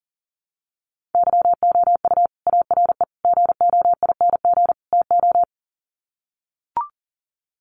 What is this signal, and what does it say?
Morse code, heard as a keyed tone, about 40 wpm, 700 Hz. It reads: YOU ARE GOING TO
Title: Sets of 4 Words 40wpm